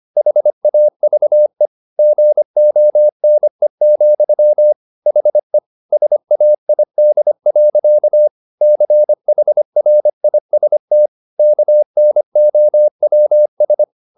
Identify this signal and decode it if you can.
Morse code, heard as a keyed tone, about 25 wpm, 600 Hz. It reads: HAVE GONE, HE SAID. CHRIST KNOWS
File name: the old man and the sea part 1 25wpm